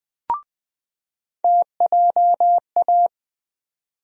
Morse code, heard as a keyed tone, about 20 wpm, 700 Hz. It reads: TJA
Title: USA Call Sign Postfixes - 3 Letters 20wpm